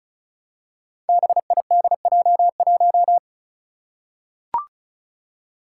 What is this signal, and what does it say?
Morse code, heard as a keyed tone, about 36 wpm, 700 Hz. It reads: BIDJ1
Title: Five Challenging Letter-Number Combinations 35wpm